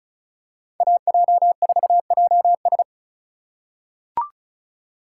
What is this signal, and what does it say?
Morse code, heard as a keyed tone, about 34 wpm, 690 Hz. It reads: AJ4JS
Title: USA Call Signs - 5 Characters - 2x Character Spacing - 35wpm